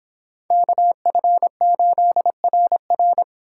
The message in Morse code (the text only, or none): KF8RR